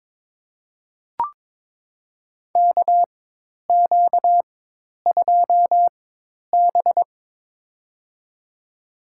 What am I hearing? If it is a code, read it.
Morse code, heard as a keyed tone, about 22 wpm, spaced as for 10 wpm, 700 Hz. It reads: KQ2B
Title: USA Call Signs - 4 Characters - 4x Character Spacing - 22wpm